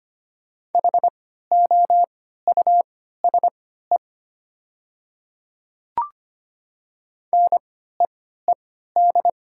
Morse code, heard as a keyed tone, about 25 wpm, spaced as for 14 wpm, 700 Hz. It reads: HOUSE NEED
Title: 3x Character Spacing - Top 500 Words - 25wpm